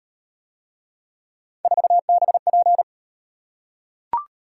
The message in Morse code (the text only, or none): 4BP